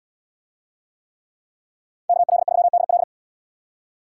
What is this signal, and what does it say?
Morse code, heard as a keyed tone, about 78 wpm, 700 Hz. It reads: KL3DL